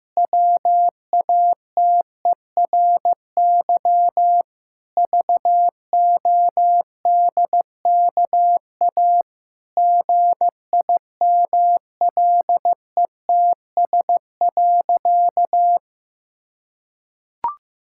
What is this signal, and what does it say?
Morse code, heard as a keyed tone, about 15 wpm, 700 Hz. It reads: WATERY VODKA GIMLETS.
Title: Pangram Challenge v2 - 15wpm